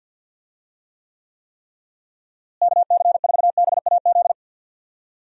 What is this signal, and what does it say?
Morse code, heard as a keyed tone, about 48 wpm, 700 Hz. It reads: KC4BAZ
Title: USA Call Signs - 6 Characters - Repeated 3x - 50wpm